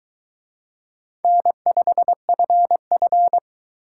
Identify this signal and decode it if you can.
Morse code, heard as a keyed tone, about 23 wpm, 700 Hz. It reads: N5FF